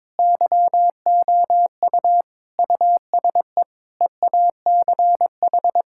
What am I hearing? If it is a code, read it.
Morse code, heard as a keyed tone, about 22 wpm, 700 Hz. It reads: YOU USE EACH
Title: Sentences from Top 1000 Words - Review Part 6 - 22wpm